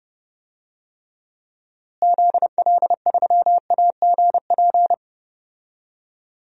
Morse code, heard as a keyed tone, about 32 wpm, 700 Hz. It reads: ZL3AGP
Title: DXCC-100 Call Signs - 6 Characters - Repeated 2x - 30wpm